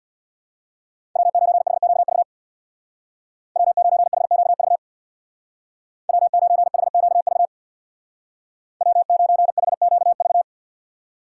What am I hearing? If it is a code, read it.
Morse code, heard as a keyed tone, about 62 wpm, 700 Hz. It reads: W9HQV W9HQV W9HQV W9HQV
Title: Speed Racing - USA Call Signs 50wpm